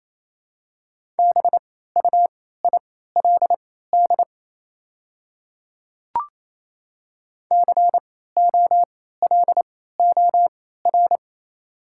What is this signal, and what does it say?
Morse code, heard as a keyed tone, about 28 wpm, spaced as for 16 wpm, 700 Hz. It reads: BUILD COLOR
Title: 3x Character Spacing - Top 300 Words - 28wpm